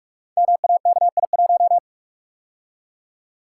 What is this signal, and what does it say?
Morse code, heard as a keyed tone, about 44 wpm, 700 Hz. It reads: MAKI1
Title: Code Groups - 5 Letter:Numbers - 45wpm